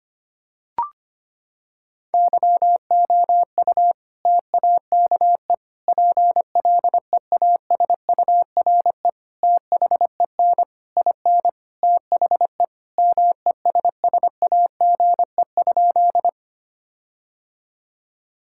Morse code, heard as a keyed tone, about 25 wpm, 700 Hz. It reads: YOU TAKE PLEASURE THEN IN THE MESSAGE?
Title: Sentences from Top 1600 Words 25wpm